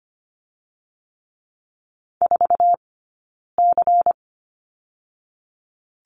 Morse code, heard as a keyed tone, about 25 wpm, 700 Hz. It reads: 4 C
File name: course / 25:10wpm / 092 Lesson 16d - Letter c - Two Characters 25wpm